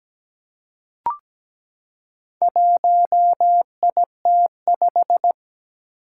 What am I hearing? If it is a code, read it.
Morse code, heard as a keyed tone, about 17 wpm, 700 Hz. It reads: 1IT5
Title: Four Challenging Letter-Number Combinations 17wpm